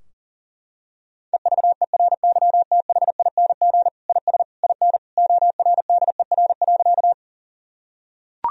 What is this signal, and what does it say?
Morse code, heard as a keyed tone, about 40 wpm, 700 Hz. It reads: EVERYTHING IS IN ORDER.
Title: Sentences from Top 1000 Words - Review Part 2 - 40wpm